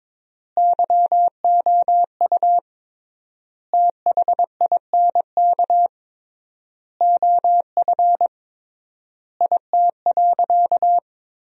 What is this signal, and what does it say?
Morse code, heard as a keyed tone, about 22 wpm, 700 Hz. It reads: YOU THINK OF IT.
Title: 3x Word-Spacing - Sentences from Top 200 Words - 22wpm